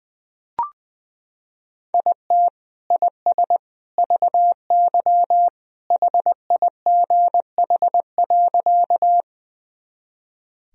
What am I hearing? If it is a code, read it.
Morse code, heard as a keyed tone, about 20 wpm, 700 Hz. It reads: IT IS VY HIGH.
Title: Sentences from Top 200 Words and Abbreviations 20wpm